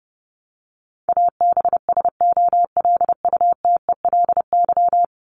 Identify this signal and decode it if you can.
Morse code, heard as a keyed tone, about 30 wpm, 700 Hz. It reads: ABSOLUTELY
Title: Top 1501-1750 Words 30wpm